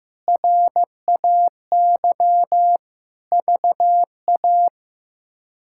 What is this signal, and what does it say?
Morse code, heard as a keyed tone, about 15 wpm, 700 Hz. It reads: RAY VA
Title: K1USN SST QSO Practice - 15wpm